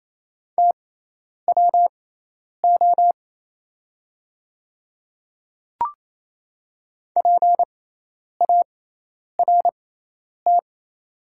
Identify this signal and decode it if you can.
Morse code, heard as a keyed tone, about 27 wpm, spaced as for 10 wpm, 700 Hz. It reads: TWO PART
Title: 6x Character Spacing - Top 200 Words - 28wpm